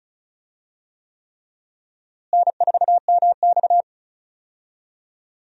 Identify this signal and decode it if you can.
Morse code, heard as a keyed tone, about 36 wpm, 700 Hz. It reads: N4MX
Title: USA Call Signs - 4 Characters - Repeated 3x - 35wpm